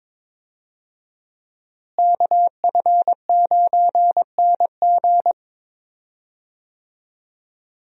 Morse code, heard as a keyed tone, about 22 wpm, 700 Hz. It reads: KF9NG
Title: USA Call Signs - 5 Characters - Repeated 3x - 22wpm